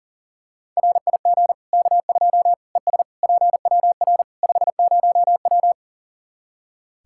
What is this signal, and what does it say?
Morse code, heard as a keyed tone, about 40 wpm, 700 Hz. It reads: RIG K2 ES PWR 50W
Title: Practice QSO - Exchange 2 - Info - 40wpm